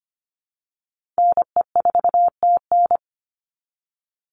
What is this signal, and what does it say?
Morse code, heard as a keyed tone, about 25 wpm, 700 Hz. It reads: NE4TN
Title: NA SOTA Chaser Callsigns 25wpm